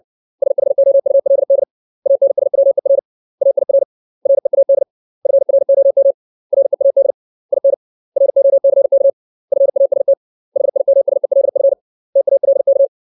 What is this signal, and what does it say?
Morse code, heard as a keyed tone, about 60 wpm, 550 Hz. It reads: SHOULD ANSWER AIR READ FROM READ IN ROCK FAST HIMSELF TALK
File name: 632 words / 632 Words 60WPM